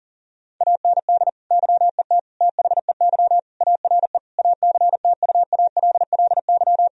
Morse code, heard as a keyed tone, about 40 wpm, 700 Hz. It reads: AND YET THEY ARE ACTUALLY
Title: Sentences from Top 1000 Words - Review Part 4 - 40wpm